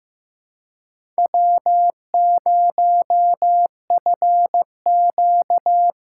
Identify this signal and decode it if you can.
Morse code, heard as a keyed tone, about 15 wpm, 700 Hz. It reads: W0FQ
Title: USA Call Signs - 4x Character Spacing - 15wpm